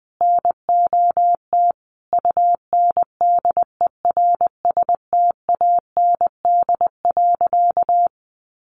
Morse code, heard as a keyed tone, about 20 wpm, 700 Hz. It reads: NOT UNDERSTAND.
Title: Sentences from Top 1500 Words 20wpm